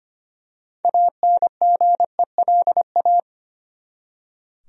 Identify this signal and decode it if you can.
Morse code, heard as a keyed tone, about 25 wpm, 700 Hz. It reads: ANGELA
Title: USA Common Names - 25wpm